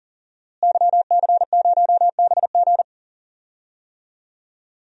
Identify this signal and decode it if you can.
Morse code, heard as a keyed tone, about 40 wpm, 700 Hz. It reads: YC0BG
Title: DXCC-100 Call Signs - 5 Characters - Repeated 2x - 40wpm